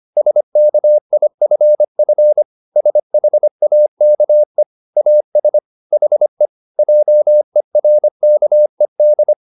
Morse code, heard as a keyed tone, about 25 wpm, 600 Hz. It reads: SKIFF SHAKE AS HE JERKED